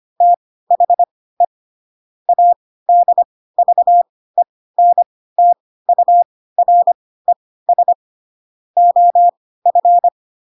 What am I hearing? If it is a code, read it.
Morse code, heard as a keyed tone, about 25 wpm, spaced as for 16 wpm, 700 Hz. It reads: THE ADVENTURES OF
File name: Pinocchio / Pinocchio ch3 10 25 ik1hge